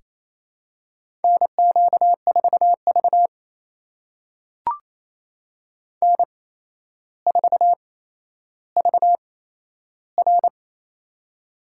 Morse code, heard as a keyed tone, about 28 wpm, spaced as for 12 wpm, 700 Hz. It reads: NQ4V N4VR
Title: USA Call Signs - 4 Characters - 8x Character Spacing - 28wpm